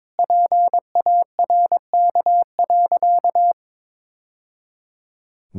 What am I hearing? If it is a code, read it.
Morse code, heard as a keyed tone, about 22 wpm, 700 Hz. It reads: PARK.
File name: practice / 22wpm / Sentences with 2-Word Phrases - 22wpm